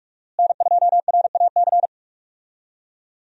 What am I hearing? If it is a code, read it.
Morse code, heard as a keyed tone, about 42 wpm, 700 Hz. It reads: N2RAC
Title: USA Call Signs - 5 Characters - Repeated 3x - 45wpm